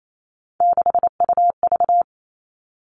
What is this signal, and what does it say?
Morse code, heard as a keyed tone, about 28 wpm, 700 Hz. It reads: 6UV
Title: All Cut Numbers - 3-Digit 28wpm